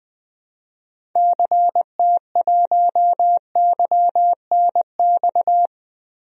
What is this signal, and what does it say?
Morse code, heard as a keyed tone, about 20 wpm, 700 Hz. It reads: CT1YNX
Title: DXCC-100 Call Signs - 6 Characters - Repeated 3x - 20wpm